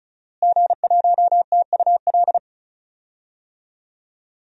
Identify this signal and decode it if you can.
Morse code, heard as a keyed tone, about 36 wpm, 700 Hz. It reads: G1TUL
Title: DXCC-100 Call Signs - 5 Characters - Repeated 3x - 35wpm